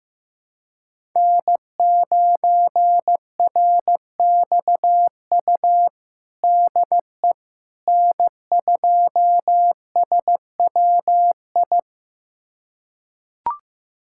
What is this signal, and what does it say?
Morse code, heard as a keyed tone, about 15 wpm, 700 Hz. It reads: N9RXU DE N2SWI
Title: USA Callsign DE Callsign - 5 Characters - 15wpm